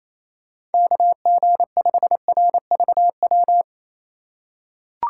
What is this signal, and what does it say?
Morse code, heard as a keyed tone, about 28 wpm, 700 Hz. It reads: KG5RVW
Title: USA Call Signs - 6 Characters - 4x Character Spacing - 28wpm